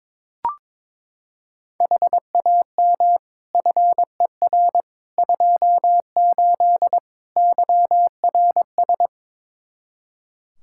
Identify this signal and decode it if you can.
Morse code, heard as a keyed tone, about 22 wpm, 700 Hz. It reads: HAM FER 28 YRS
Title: Practice QSO - Exchange 3 - More Info - 22wpm